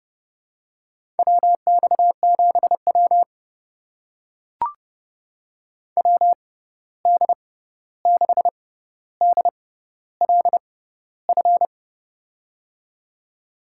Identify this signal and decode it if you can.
Morse code, heard as a keyed tone, about 32 wpm, spaced as for 14 wpm, 700 Hz. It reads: WX7W WD6DLF
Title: USA Call Signs - 6x Character Spacing - 30wpm